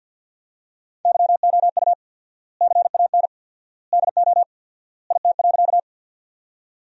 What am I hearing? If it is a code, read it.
Morse code, heard as a keyed tone, about 50 wpm, 700 Hz. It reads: YOU CAN DO IT.
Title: 4x Word-Spacing - Sentences from Top 100 Words - 50wpm